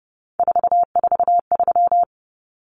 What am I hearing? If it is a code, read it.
Morse code, heard as a keyed tone, about 30 wpm, 700 Hz. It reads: 443
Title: Numbers - 3-Digit 30wpm